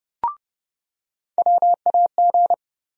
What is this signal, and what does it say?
Morse code, heard as a keyed tone, about 30 wpm, 700 Hz. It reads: WAG
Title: USA Call Sign Postfixes - 3 Letters 30wpm